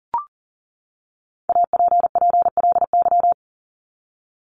Morse code, heard as a keyed tone, about 40 wpm, 700 Hz. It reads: APPLY
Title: ED Suffix - Top 100 Words with Base Words 40wpm